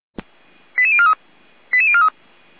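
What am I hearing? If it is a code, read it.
Five-tone selective call (ZVEI), heard as quick runs of five tones, tones of about 70 ms. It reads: 88043 88043
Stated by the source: ZVEI five-tone call received by SDR.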